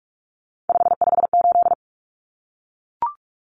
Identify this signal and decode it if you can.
Morse code, heard as a keyed tone, about 44 wpm, 700 Hz. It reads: 558
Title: Common Cut Numbers - 3-Digit 45wpm